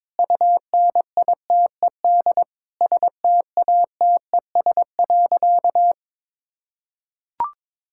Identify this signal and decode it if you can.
Morse code, heard as a keyed tone, about 22 wpm, 700 Hz. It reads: UNITED STATES.
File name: practice / 22wpm / Sentences with 8-Word Phrases - 22wpm